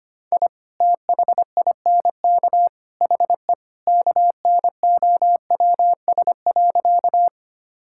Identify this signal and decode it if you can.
Morse code, heard as a keyed tone, about 25 wpm, 700 Hz. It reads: I THINK HE KNOWS.
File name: practice / 25wpm / Sentences with 3-Word Phrases - 25wpm